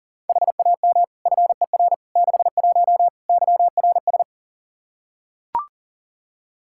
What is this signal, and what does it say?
Morse code, heard as a keyed tone, about 40 wpm, 700 Hz. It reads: HAM FER 61 YRS
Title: K1USN SST QSO Practice 40wpm